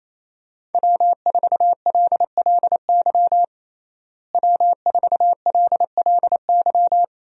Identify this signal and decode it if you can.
Morse code, heard as a keyed tone, about 28 wpm, 700 Hz. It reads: W4LLY W4LLY
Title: USA Call Signs - 5 Characters - Repeated 2x - 28wpm